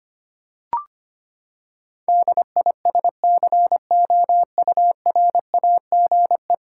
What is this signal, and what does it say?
Morse code, heard as a keyed tone, about 25 wpm, 700 Hz. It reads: DISCOURAGE
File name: practice / 25wpm / Ten Letter Words - 25wpm